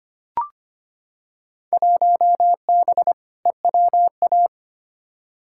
Morse code, heard as a keyed tone, about 25 wpm, 700 Hz. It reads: 1B EWA
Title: Field Day Exchanges Encore - 25wpm